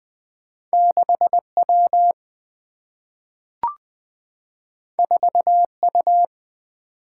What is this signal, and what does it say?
Morse code, heard as a keyed tone, about 20 wpm, 700 Hz. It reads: 6 W 4 U
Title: Single-Letter Single-Number 20wpm